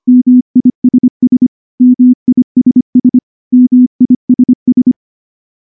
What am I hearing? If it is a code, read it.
Morse code, heard as a keyed tone, about 23 wpm, 250 Hz. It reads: MISS MISS MISS
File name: morse / morse-training-25WPM